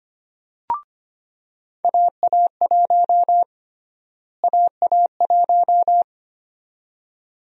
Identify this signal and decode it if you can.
Morse code, heard as a keyed tone, about 25 wpm, 700 Hz. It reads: AA1 AA1
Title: USA Call Sign Prefixes - Repeated 2x - 25wpm